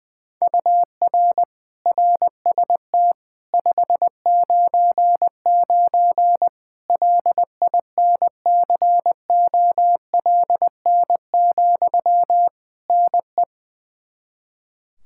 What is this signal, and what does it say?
Morse code, heard as a keyed tone, about 20 wpm, 700 Hz. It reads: UR RST 599 LINCOLN, NE
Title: Practice QSO - Exchange 1 - Beginning - 20wpm